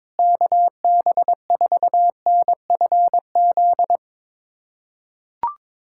Morse code, heard as a keyed tone, about 22 wpm, 700 Hz. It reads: KB4NFZ
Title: USA Callsign DE Callsign 22wpm